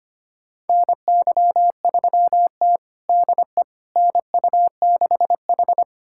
Morse code, heard as a keyed tone, about 25 wpm, 700 Hz. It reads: NY3T DE NU6H
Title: USA Callsign DE Callsign - 4 Characters - 25wpm